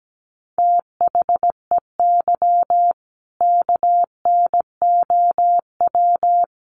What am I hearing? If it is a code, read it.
Morse code, heard as a keyed tone, about 17 wpm, 700 Hz. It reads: THEY KNOW
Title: Sentences from Top 500 Words 17wpm